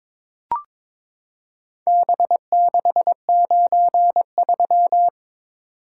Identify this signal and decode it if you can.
Morse code, heard as a keyed tone, about 22 wpm, 700 Hz. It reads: B693
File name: practice / 22wpm / Four Letter-Number Combinations - 22wpm